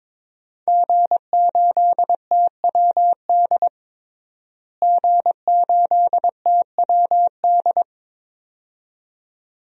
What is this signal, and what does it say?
Morse code, heard as a keyed tone, about 22 wpm, 700 Hz. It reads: G8TWD G8TWD
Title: DXCC-100 Call Signs - 5 Characters - Repeated 2x - 22wpm